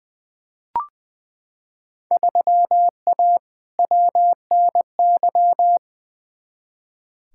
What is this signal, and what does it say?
Morse code, heard as a keyed tone, about 20 wpm, 700 Hz. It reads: 3A WNY
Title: Field Day Exchanges - 20wpm